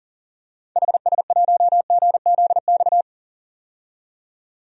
Morse code, heard as a keyed tone, about 40 wpm, 700 Hz. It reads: HS1GZX
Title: DXCC-100 Call Signs - 6 Characters - 40wpm